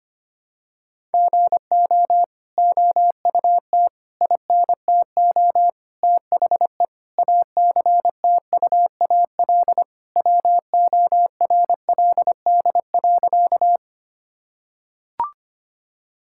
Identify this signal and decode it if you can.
Morse code, heard as a keyed tone, about 25 wpm, 700 Hz. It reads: GO OUT INTO THE ACTUAL WORLD.
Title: Sentences from Top 1000 Words - Review Part 5 - 25wpm